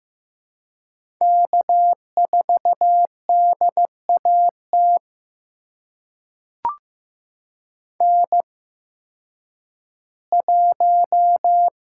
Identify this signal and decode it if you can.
Morse code, heard as a keyed tone, about 15 wpm, spaced as for 10 wpm, 700 Hz. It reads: K4DAT N1
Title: USA Call Signs - 5 Characters - 8x Character Spacing - 15wpm